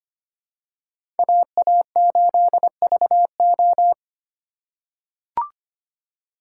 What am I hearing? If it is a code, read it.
Morse code, heard as a keyed tone, about 25 wpm, 700 Hz. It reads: AA8VO
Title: USA Call Signs - 5 Characters - 3x Character Spacing - 25wpm